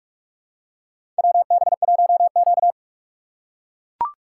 Morse code, heard as a keyed tone, about 42 wpm, 700 Hz. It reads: WB1Q
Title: USA Call Signs - 6x Character Spacing - 45wpm